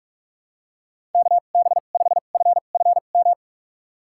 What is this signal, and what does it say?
Morse code, heard as a keyed tone, about 46 wpm, spaced as for 32 wpm, 700 Hz. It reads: KB5FFM
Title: USA Call Signs - 6 Characters - 2x Character Spacing - 45wpm